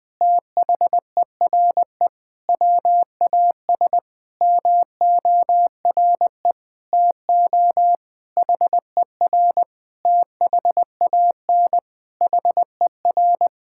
Morse code, heard as a keyed tone, about 20 wpm, 700 Hz. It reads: THERE WAS MORE TO HER THAN HER
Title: Sentences from Top 1000 Words - Review Part 6 - 20wpm